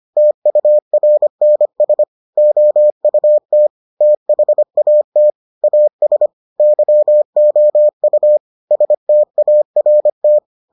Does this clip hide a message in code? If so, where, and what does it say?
Yes: Morse code from 0.2 to 10.4 s, heard as a keyed tone, about 25 wpm, 600 Hz. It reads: TURNS OUT THAT AS YOU START